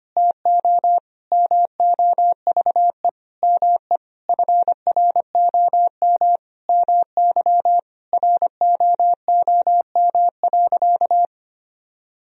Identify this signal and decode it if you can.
Morse code, heard as a keyed tone, about 25 wpm, 700 Hz. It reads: TO MOVE ME FROM MY ROOM.
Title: Sentences from Top 1000 Words - Review Part 1 - 25wpm